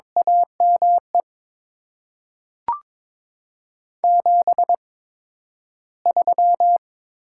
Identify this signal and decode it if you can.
Morse code, heard as a keyed tone, about 21 wpm, spaced as for 12 wpm, 700 Hz. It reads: AME 73
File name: Top 20 QSO Elements - 8x Character Spacing - 22wpm